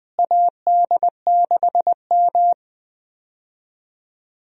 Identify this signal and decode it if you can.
Morse code, heard as a keyed tone, about 20 wpm, 700 Hz. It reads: AD6M
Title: USA Call Signs - 4 Characters - Repeated 2x - 20wpm